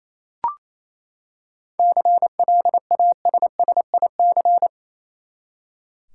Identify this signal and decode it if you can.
Morse code, heard as a keyed tone, about 28 wpm, 700 Hz. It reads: CLASSIC
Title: Seven Letter Words - 28wpm